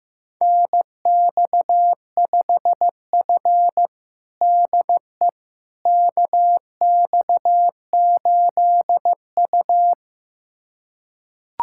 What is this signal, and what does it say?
Morse code, heard as a keyed tone, about 16 wpm, 700 Hz. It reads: NX5F DE KX8U
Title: USA Callsign DE Callsign - 4 Characters - 15wpm